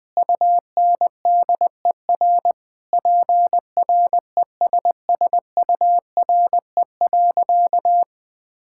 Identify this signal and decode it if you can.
Morse code, heard as a keyed tone, about 20 wpm, 700 Hz. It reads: UNDER PRESSURE.
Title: All-Age Appropriate Jokes - 20wpm